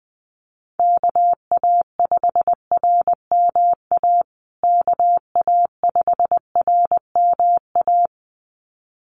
Morse code, heard as a keyed tone, about 20 wpm, 700 Hz. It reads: KA5RMA KA5RMA
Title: USA Call Signs Repeated 2x 20wpm